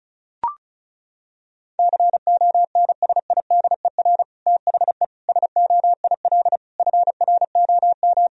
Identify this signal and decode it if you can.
Morse code, heard as a keyed tone, about 36 wpm, 700 Hz. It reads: CONSIDER THE SOIL FROM
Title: Sentences from Top 1000 Words - Review Part 4 - 35wpm